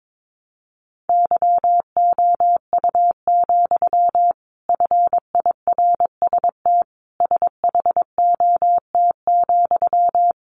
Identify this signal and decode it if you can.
Morse code, heard as a keyed tone, about 22 wpm, 700 Hz. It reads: YOU, FIRST SHOT,
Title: A Sound Of Thunder 22wpm